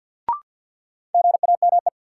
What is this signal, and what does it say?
Morse code, heard as a keyed tone, about 52 wpm, 700 Hz. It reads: GAME
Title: Sentences from Top 500 Words - New Word Drill - 50wpm